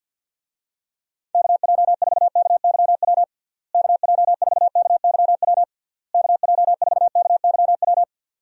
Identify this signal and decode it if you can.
Morse code, heard as a keyed tone, about 52 wpm, 700 Hz. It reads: KJ4KYW KJ4KYW KJ4KYW
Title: USA Call Signs - 6 Characters - Repeated 3x - 50wpm